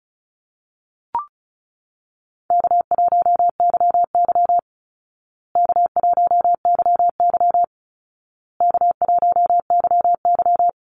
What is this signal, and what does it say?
Morse code, heard as a keyed tone, about 34 wpm, 700 Hz. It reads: K1YY K1YY K1YY
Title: Easy Call Signs Repeated 3x - 35wpm